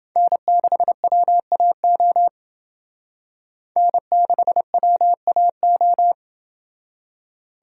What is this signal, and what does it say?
Morse code, heard as a keyed tone, about 28 wpm, 700 Hz. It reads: N6WAO N6WAO